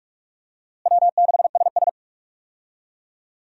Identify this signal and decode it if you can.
Morse code, heard as a keyed tone, about 46 wpm, 700 Hz. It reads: W6SS
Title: USA Call Signs - 4 Characters - Repeated 2x - 45wpm